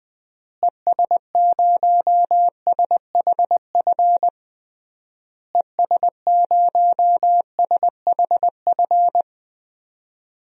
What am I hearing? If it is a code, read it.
Morse code, heard as a keyed tone, about 20 wpm, 700 Hz. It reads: ES0SHF ES0SHF
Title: DXCC-100 Call Signs - 6 Characters - Repeated 3x - 20wpm